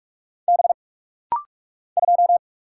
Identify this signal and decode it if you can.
Morse code, heard as a keyed tone, about 46 wpm, 700 Hz. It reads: B 2
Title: Single Letter-Number - Rapid-Fire 45wpm